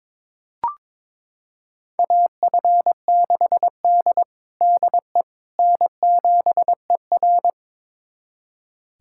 Morse code, heard as a keyed tone, about 21 wpm, 700 Hz. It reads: AF6D DE N7ER
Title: USA Callsign DE Callsign - 4 Characters - 22wpm